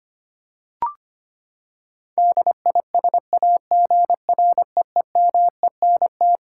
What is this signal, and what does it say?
Morse code, heard as a keyed tone, about 25 wpm, 700 Hz. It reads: DISAGREEMENT
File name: Twelve Letter Words - 25wpm